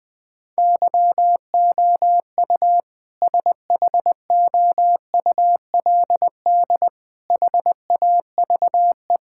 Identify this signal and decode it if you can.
Morse code, heard as a keyed tone, about 20 wpm, 700 Hz. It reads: YOU SHOULD HAVE
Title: Sentences from Top 1600 Words 20wpm